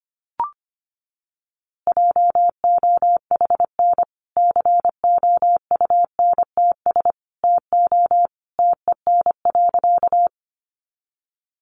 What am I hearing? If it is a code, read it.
Morse code, heard as a keyed tone, about 25 wpm, 700 Hz. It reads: JOHN COUNTS TO TEN.
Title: Easy Phrases 25wpm